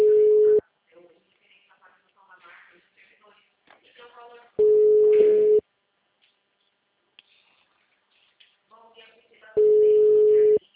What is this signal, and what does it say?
Ringback tone (Europe, 425 Hz), with speech.
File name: out- -2007- - - .49